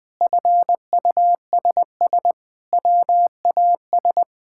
Morse code, heard as a keyed tone, about 20 wpm, 700 Hz. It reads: FUSS WAS
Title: Sentences with 8-Word Phrases - Spoken Beginning - 20wpm